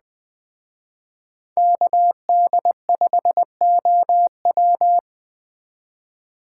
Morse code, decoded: KD5OW

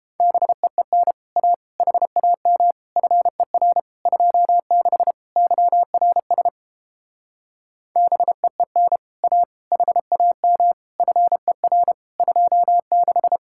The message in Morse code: BEEN A HAM FER 26 YRS BEEN A HAM FER 26

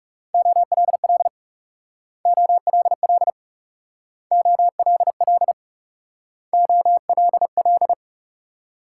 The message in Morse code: OLL OLL OLL OLL

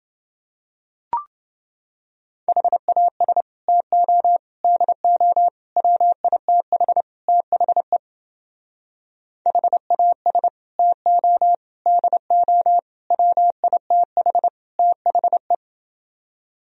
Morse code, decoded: HAS TO DO WITH THE HAS TO DO WITH THE

